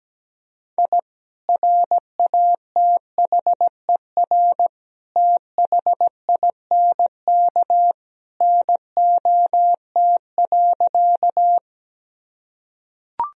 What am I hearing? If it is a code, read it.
Morse code, heard as a keyed tone, about 17 wpm, 700 Hz. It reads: I RATHER THINK NOT.
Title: Sentences from Top 1000 Words - Review Part 4 - 17wpm